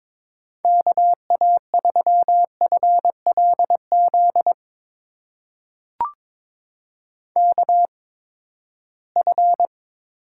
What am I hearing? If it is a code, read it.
Morse code, heard as a keyed tone, about 22 wpm, spaced as for 15 wpm, 700 Hz. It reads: KA3FLZ KF
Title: USA Call Signs - 8x Character Spacing - 22wpm